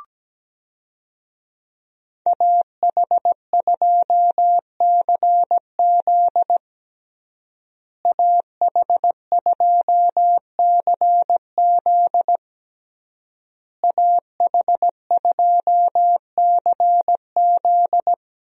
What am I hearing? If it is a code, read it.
Morse code, heard as a keyed tone, about 17 wpm, 700 Hz. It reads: AH2CZ AH2CZ AH2CZ
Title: USA Call Signs - 5 Characters - Repeated 3x - 17wpm